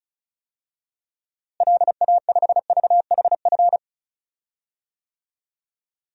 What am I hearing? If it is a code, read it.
Morse code, heard as a keyed tone, about 36 wpm, 700 Hz. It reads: LA5VHF
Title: DXCC-100 Call Signs - 6 Characters - Repeated 3x - 35wpm